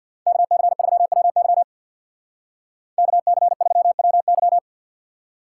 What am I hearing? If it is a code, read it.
Morse code, heard as a keyed tone, about 60 wpm, 700 Hz. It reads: KC3WY KC3WY
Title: Speed Racing - USA 2x2 Call Signs 50wpm